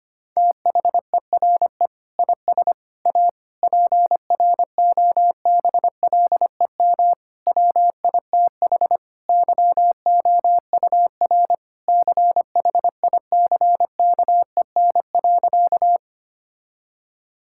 Morse code, heard as a keyed tone, about 25 wpm, 700 Hz. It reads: THERE IS A PROBLEM WITH YOUR CHICKEN.
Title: Sentences from Top 1000 Words - Review Part 5 - 25wpm